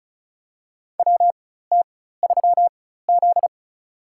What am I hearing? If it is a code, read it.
Morse code, heard as a keyed tone, about 34 wpm, spaced as for 16 wpm, 700 Hz. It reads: WT3Z
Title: USA Call Signs - 4 Characters - 4x Character Spacing - 35wpm